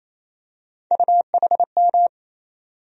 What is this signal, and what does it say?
Morse code, heard as a keyed tone, about 28 wpm, 700 Hz. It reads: UHM
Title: USA Call Sign Postfixes - 3 Letters 28wpm